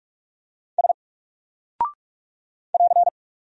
Morse code, heard as a keyed tone, about 44 wpm, 700 Hz. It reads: S <AR>
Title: Single Character (Letter:Number:Punctuation:Prosign) 45wpm